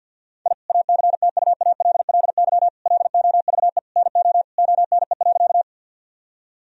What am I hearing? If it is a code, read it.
Morse code, heard as a keyed tone, about 50 wpm, 700 Hz. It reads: I ACTUALLY LOVE NO ONE.
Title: Sentences from Top 1000 Words - Review Part 4 - 50wpm